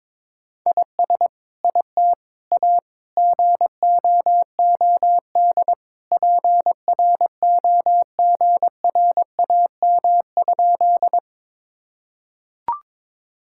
Morse code, heard as a keyed tone, about 22 wpm, 700 Hz. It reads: IS IT A GOOD PROGRAM?
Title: Sentences from Top 1000 Words - Review Part 5 - 22wpm